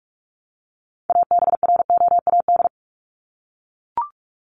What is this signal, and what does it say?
Morse code, heard as a keyed tone, about 44 wpm, 700 Hz. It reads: ABROAD
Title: Top 1751-2000 Words 45wpm